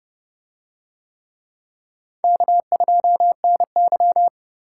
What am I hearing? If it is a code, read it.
Morse code, heard as a keyed tone, about 29 wpm, 700 Hz. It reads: K2NY